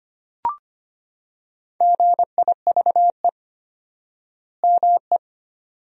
Morse code, heard as a keyed tone, about 25 wpm, 700 Hz. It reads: GIVE ME
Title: 4x Word-Spacing - Sentences from Top 200 Words - 25wpm